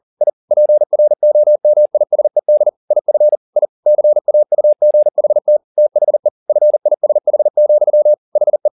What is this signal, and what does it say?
Morse code, heard as a keyed tone, about 40 wpm, 600 Hz. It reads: I PROMISED IF I CAUGHT THE FISH, HE
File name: the old man and the sea 2 40wpm